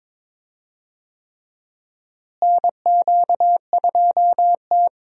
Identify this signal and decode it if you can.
Morse code, heard as a keyed tone, about 22 wpm, 700 Hz. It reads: NQ2T
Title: USA Call Signs - 4 Characters - Repeated 3x - 22wpm